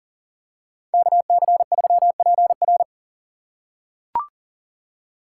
Morse code, heard as a keyed tone, about 40 wpm, 700 Hz. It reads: KC3PR
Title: USA Call Signs - 5 Characters - 4x Character Spacing - 40wpm